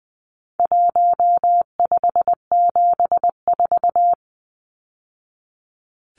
Morse code, heard as a keyed tone, about 20 wpm, 700 Hz. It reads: 1574
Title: Numbers - 4-Digit 20wpm